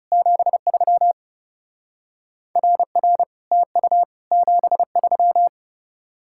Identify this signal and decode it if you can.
Morse code, heard as a keyed tone, about 36 wpm, 700 Hz. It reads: 73 RR TU 73
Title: Speed-Racing Common QSO Phrases Encore - 30wpm